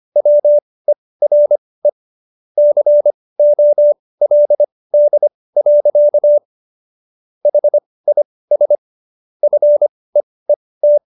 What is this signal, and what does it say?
Morse code, heard as a keyed tone, about 25 wpm, spaced as for 18 wpm, 590 Hz. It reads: WERE COLD. HIS FEET